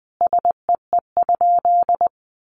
Morse code, heard as a keyed tone, about 20 wpm, 700 Hz. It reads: SEE?
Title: Sentences from Top 1300 Words 20wpm